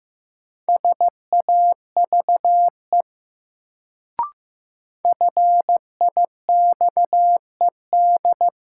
Morse code, heard as a keyed tone, about 15 wpm, 700 Hz. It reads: SAVE FIXED
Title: Sentences from Top 1100 Words - New Word Drill Alt - 15wpm